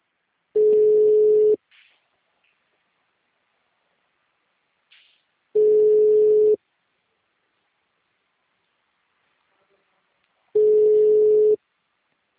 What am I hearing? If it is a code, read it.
Ringback tone (Europe, 425 Hz).